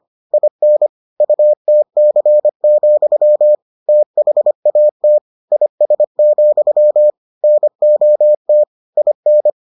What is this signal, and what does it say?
Morse code, heard as a keyed tone, about 25 wpm, 600 Hz. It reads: IN UTC, THAT IS, NOT IN